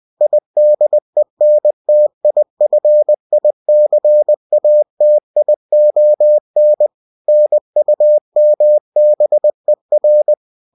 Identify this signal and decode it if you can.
Morse code, heard as a keyed tone, about 20 wpm, 600 Hz. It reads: IDENTIFICATION NUMBER